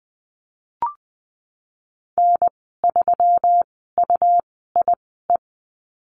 Morse code, heard as a keyed tone, about 20 wpm, spaced as for 15 wpm, 700 Hz. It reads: N3UIE
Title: 161 Lesson 26g - Letter g - Call Signs 20wpm